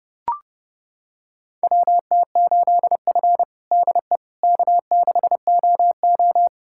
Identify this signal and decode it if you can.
Morse code, heard as a keyed tone, about 29 wpm, 700 Hz. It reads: WT8F DE K6OO
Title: USA Callsign DE Callsign - 4 Characters - 30wpm